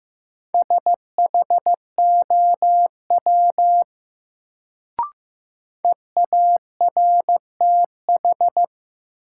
Morse code, heard as a keyed tone, about 15 wpm, 700 Hz. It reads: SHOW EARTH
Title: Sentences from Top 200 Words - New Word Drill - 15wpm